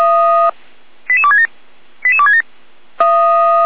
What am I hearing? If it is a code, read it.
Five-tone selective call (ZVEI), heard as quick runs of five tones, tones of about 70 ms. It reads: 88267 88267